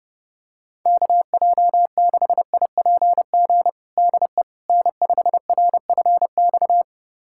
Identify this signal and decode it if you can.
Morse code, heard as a keyed tone, about 30 wpm, 700 Hz. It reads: KJ6IPG DE N5RFX
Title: USA Callsign DE Callsign 30wpm